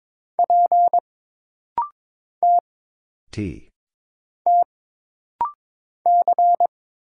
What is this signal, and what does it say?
Morse code, heard as a keyed tone, about 22 wpm, 700 Hz. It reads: P T T C
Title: Single Letter-Number Alt - Rapid-Fire 22wpm